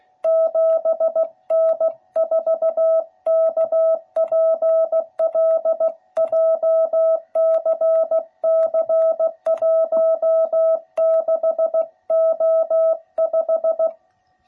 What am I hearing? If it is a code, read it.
Morse code, heard as a keyed tone, about 15 wpm, 660 Hz. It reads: 7N4KPL JCC16O5
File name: jh8cbh / 2021cw / cw12b